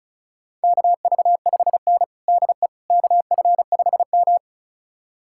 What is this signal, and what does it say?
Morse code, heard as a keyed tone, about 36 wpm, 700 Hz. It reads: KV5N DE KF5M
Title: USA Callsign DE Callsign - 4 Characters - 35wpm